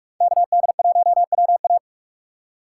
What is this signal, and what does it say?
Morse code, heard as a keyed tone, about 46 wpm, 700 Hz. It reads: KD1WA
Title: USA Call Signs - 4x Character Spacing - 45wpm